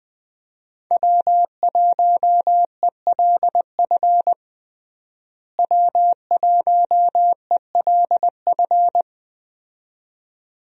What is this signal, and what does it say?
Morse code, heard as a keyed tone, about 21 wpm, 700 Hz. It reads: W1ELF W1ELF
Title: USA Call Signs - 5 Characters - Repeated 2x - 20wpm